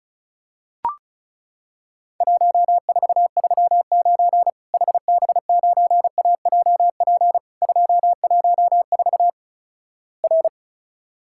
Morse code, heard as a keyed tone, about 36 wpm, 700 Hz. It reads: 1439 HB9AJP 214 R
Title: Worked All Europe DX Contest - QTC Exchange - 35wpm